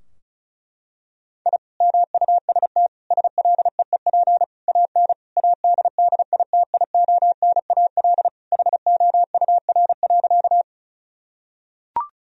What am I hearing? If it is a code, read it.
Morse code, heard as a keyed tone, about 36 wpm, 700 Hz. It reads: I MUST SLEEP AN ADDITIONAL HOUR.
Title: Sentences from Top 1000 Words - Review Part 4 - 35wpm